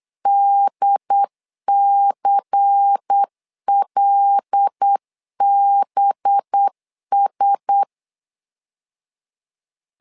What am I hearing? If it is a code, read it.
Morse code, heard as a keyed tone, about 8 wpm, 800 Hz. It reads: DCLBS
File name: LEZIONI / 5 LBVC / 5 LBVC 8wpm 2935